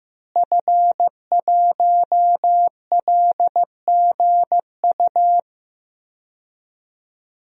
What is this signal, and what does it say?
Morse code, heard as a keyed tone, about 15 wpm, 700 Hz. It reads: F1LGU
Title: DXCC-100 Call Signs - 5 Characters - Repeated 2x - 15wpm